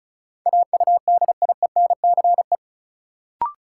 Morse code, heard as a keyed tone, about 34 wpm, 700 Hz. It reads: AUDIENCE
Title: Sentences from Top 800 Words - New Word Drill - 35wpm